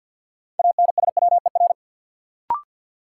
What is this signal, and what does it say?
Morse code, heard as a keyed tone, about 53 wpm, 700 Hz. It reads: ANSWER
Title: Sentences from Top 300 Words - New Word Drill - 50wpm